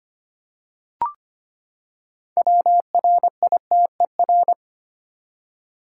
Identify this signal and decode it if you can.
Morse code, heard as a keyed tone, about 25 wpm, 700 Hz. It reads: WRITER
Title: Six Letter Words - 25wpm